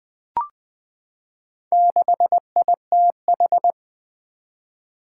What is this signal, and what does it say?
Morse code, heard as a keyed tone, about 20 wpm, 700 Hz. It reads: 6ITH
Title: Four Challenging Letter-Number Combinations 20wpm